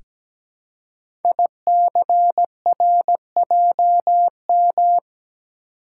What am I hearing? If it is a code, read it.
Morse code, heard as a keyed tone, about 17 wpm, 700 Hz. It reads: ICRJM
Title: Code Groups - 5 Letters - 17wpm